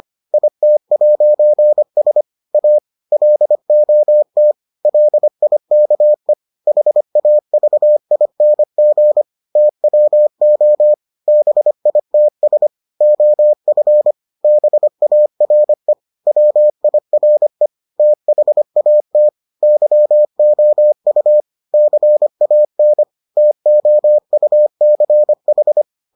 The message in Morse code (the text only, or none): IT'S A LOT LIKE HAVING TWO BITS OF BARE WIRE THAT YOU CAN TOUCH